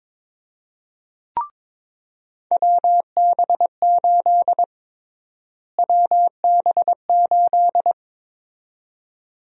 Morse code, heard as a keyed tone, about 22 wpm, 700 Hz. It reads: WB8 WB8
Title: USA Call Sign Prefixes - Repeated 2x - 22wpm